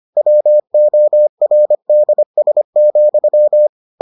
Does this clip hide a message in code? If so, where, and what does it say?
Yes: Morse code from 0.2 to 3.7 s, heard as a keyed tone, about 25 wpm, 600 Hz. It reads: WORDS,